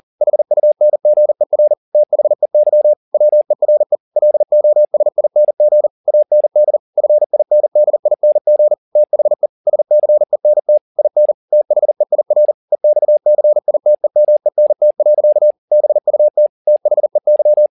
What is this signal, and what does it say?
Morse code, heard as a keyed tone, about 40 wpm, 600 Hz. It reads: HUNGER THEY WERE LOSING AND FINDING THE SCENT IN THEIR EXCITEMENT. BUT THEY
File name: the old man and the sea 2 40wpm